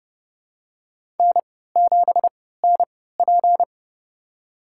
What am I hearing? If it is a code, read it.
Morse code, heard as a keyed tone, about 30 wpm, spaced as for 17 wpm, 700 Hz. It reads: N7NP
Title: USA Call Signs - 4 Characters - 3x Character Spacing - 30wpm